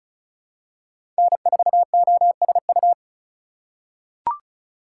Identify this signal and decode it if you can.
Morse code, heard as a keyed tone, about 34 wpm, 700 Hz. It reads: N4OSU
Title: USA Call Signs - 5 Characters - 2x Character Spacing - 35wpm